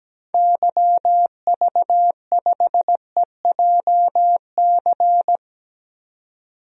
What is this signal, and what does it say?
Morse code, heard as a keyed tone, about 17 wpm, 700 Hz. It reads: YV5EJC
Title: DXCC-100 Call Signs - 6 Characters - 17wpm